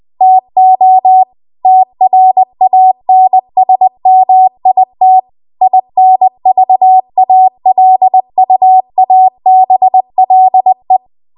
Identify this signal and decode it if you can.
Morse code, heard as a keyed tone, about 20 wpm, 750 Hz. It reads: TO TRANSMIT INVALUABLE